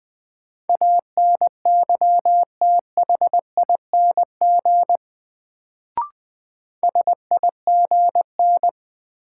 Morse code, heard as a keyed tone, about 20 wpm, 700 Hz. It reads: ANYTHING SIGN
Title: Sentences from Top 1000 Words - New Word Drill - 20wpm